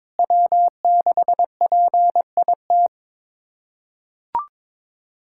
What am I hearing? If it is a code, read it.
Morse code, heard as a keyed tone, about 22 wpm, 700 Hz. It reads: W6PIT
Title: USA Callsign DE Callsign - 5 Characters - 22wpm